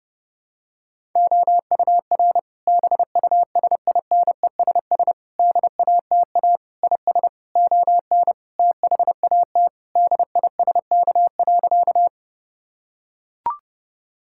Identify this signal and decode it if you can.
Morse code, heard as a keyed tone, about 30 wpm, 700 Hz. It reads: OUR BUSINESS DATA IS ON THAT DISK.
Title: Sentences from Top 1000 Words - Review Part 6 - 30wpm